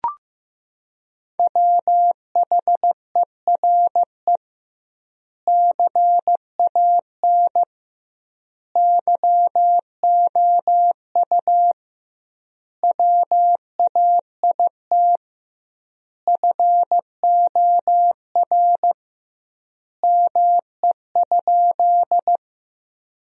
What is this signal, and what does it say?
Morse code, heard as a keyed tone, about 16 wpm, 700 Hz. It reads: WHERE CAN YOU WAIT FOR ME?
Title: 2x Word-Spacing - Sentences from Top 500 Words - 15wpm